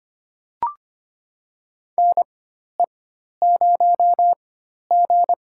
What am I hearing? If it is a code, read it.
Morse code, heard as a keyed tone, about 25 wpm, spaced as for 12 wpm, 700 Hz. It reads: NE0G
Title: USA Call Signs - 4 Characters - 4x Character Spacing - 25wpm